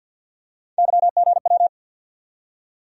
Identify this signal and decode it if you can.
Morse code, heard as a keyed tone, about 52 wpm, 690 Hz. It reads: YGW
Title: USA Call Sign Postfixes - 3 Letters 50wpm